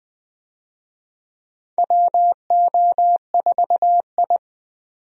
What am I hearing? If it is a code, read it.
Morse code, heard as a keyed tone, about 20 wpm, 700 Hz. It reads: WO4I
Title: USA Call Signs - 4 Characters - 20wpm